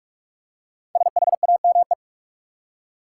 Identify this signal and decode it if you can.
Morse code, heard as a keyed tone, about 42 wpm, 700 Hz. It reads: SHAME
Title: Five Letter Words - 45wpm